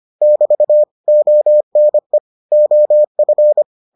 Morse code, heard as a keyed tone, about 25 wpm, 600 Hz. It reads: <BT> ONE OF